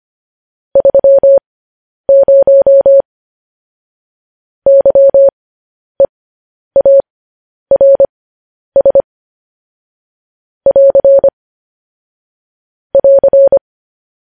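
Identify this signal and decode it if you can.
Morse code, heard as a keyed tone, about 25 wpm, spaced as for 10 wpm, 550 Hz. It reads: 30 YEARS <AR> <AR>
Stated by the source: Morse code transmission of news headlines.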